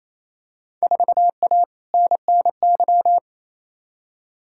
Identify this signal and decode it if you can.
Morse code, heard as a keyed tone, about 28 wpm, 700 Hz. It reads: 4A NNY
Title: Field Day Exchanges Encore - 28wpm